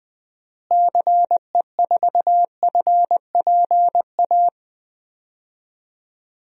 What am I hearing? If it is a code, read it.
Morse code, heard as a keyed tone, about 20 wpm, 700 Hz. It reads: CE4FPA
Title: DXCC-100 Call Signs - 6 Characters - Repeated 2x - 20wpm